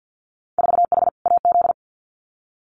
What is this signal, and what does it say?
Morse code, heard as a keyed tone, about 50 wpm, 700 Hz. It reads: 4H AZ
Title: Winter Field Day Exchanges Encore 50wpm